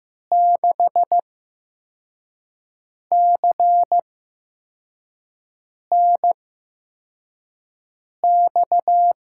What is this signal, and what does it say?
Morse code, heard as a keyed tone, about 15 wpm, spaced as for 4 wpm, 700 Hz. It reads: 6CNX
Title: USA Call Signs - 6 Characters - 8x Character Spacing - 15wpm